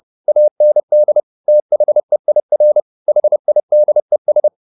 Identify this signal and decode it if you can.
Morse code, heard as a keyed tone, about 30 wpm, 600 Hz. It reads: AND THEIR HIDES
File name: the old man and the sea / the old man and the sea 1 30wpm